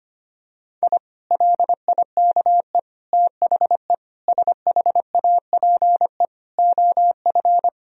Morse code, heard as a keyed tone, about 25 wpm, 700 Hz. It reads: I LIKE THE SHAPE OF
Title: Sentences from Top 1000 Words - Review Part 3 - 25wpm